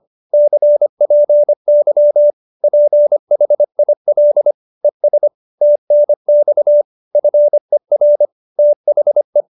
Morse code, heard as a keyed tone, about 25 wpm, 590 Hz. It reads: CPY PHIL ES TNX FER THE
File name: QSO315 25